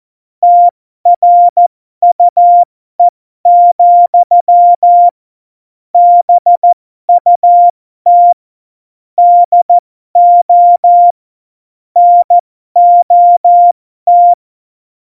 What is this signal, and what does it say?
Morse code, heard as a keyed tone, about 14 wpm, spaced as for 12 wpm, 700 Hz. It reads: TRUE, BUT DO NOT
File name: Morsekurs / Witze